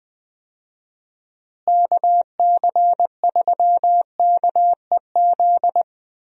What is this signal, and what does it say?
Morse code, heard as a keyed tone, about 20 wpm, 700 Hz. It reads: KC3KEZ
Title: USA Call Signs - 6 Characters - Repeated 3x - 20wpm